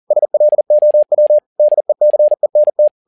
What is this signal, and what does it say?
Morse code, heard as a keyed tone, about 40 wpm, 600 Hz. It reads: SLOW DECENT
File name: learning center / tower trouble / tower touble 40wpm